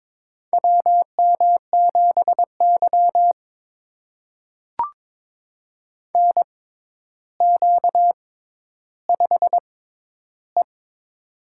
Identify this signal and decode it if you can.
Morse code, heard as a keyed tone, about 22 wpm, spaced as for 11 wpm, 700 Hz. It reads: WM7Y NQ5E
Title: USA Call Signs - 4 Characters - 6x Character Spacing - 22wpm